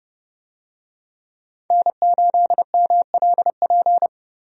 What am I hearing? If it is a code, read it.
Morse code, heard as a keyed tone, about 30 wpm, 700 Hz. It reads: N8MLP
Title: USA Call Signs - 5 Characters - 30wpm